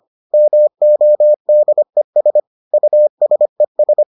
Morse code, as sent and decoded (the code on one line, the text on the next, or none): -- --- -.. ... / ..- ... ...
MODES USES